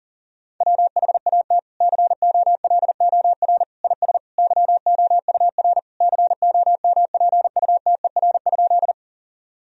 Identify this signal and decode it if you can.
Morse code, heard as a keyed tone, about 37 wpm, 700 Hz. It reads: WHAT COLOR IS YOUR COMPUTER?
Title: Sentences from Top 1000 Words - Review Part 3 - 40wpm